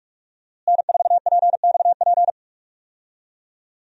Morse code, heard as a keyed tone, about 46 wpm, 700 Hz. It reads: N4PXP
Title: USA Call Signs - 5 Characters - Repeated 3x - 45wpm